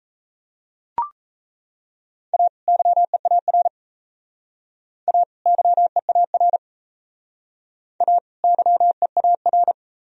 Morse code, heard as a keyed tone, about 34 wpm, 700 Hz. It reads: A YEAR A YEAR A YEAR
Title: Speed-Racing - Sets of 2 Words 28wpm